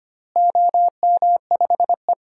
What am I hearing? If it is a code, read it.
Morse code, heard as a keyed tone, about 25 wpm, 700 Hz. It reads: OM5E
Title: DXCC-100 Call Signs - 4 Characters - Repeated 2x - 25wpm